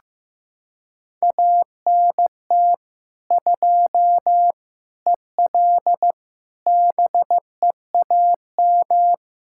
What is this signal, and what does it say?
Morse code, heard as a keyed tone, about 15 wpm, 700 Hz. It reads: ANT 2 EL BEAM
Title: K1USN SST QSO Practice 15wpm